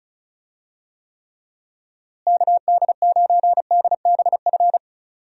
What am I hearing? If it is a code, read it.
Morse code, heard as a keyed tone, about 36 wpm, 700 Hz. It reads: KD9DBF
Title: USA Call Signs - 6 Characters - Repeated 2x - 35wpm